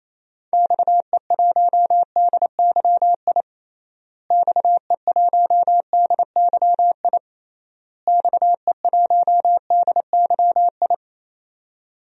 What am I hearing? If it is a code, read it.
Morse code, heard as a keyed tone, about 28 wpm, 700 Hz. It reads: XE1DYI XE1DYI XE1DYI
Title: DXCC-100 Call Signs - 6 Characters - Repeated 3x - 28wpm